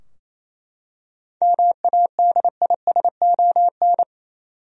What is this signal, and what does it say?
Morse code, heard as a keyed tone, about 27 wpm, 700 Hz. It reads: MADISON
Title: USA Common Names - 28wpm